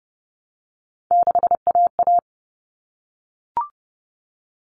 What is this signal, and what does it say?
Morse code, heard as a keyed tone, about 30 wpm, 700 Hz. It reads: 6AA
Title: All Cut Numbers - 3-Digit 30wpm